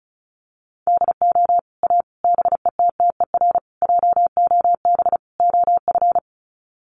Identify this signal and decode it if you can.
Morse code, heard as a keyed tone, about 34 wpm, 700 Hz. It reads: DO A BETTER JOB OF
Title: Sets of 5 Words Encore2 35wpm